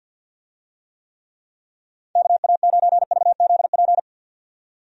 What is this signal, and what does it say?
Morse code, heard as a keyed tone, about 51 wpm, 700 Hz. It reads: KA9VZP
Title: USA Call Signs - 6 Characters - 50wpm